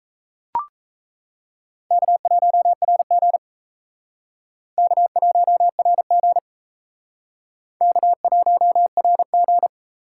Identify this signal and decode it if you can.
Morse code, heard as a keyed tone, about 37 wpm, 700 Hz. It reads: K1RG K1RG K1RG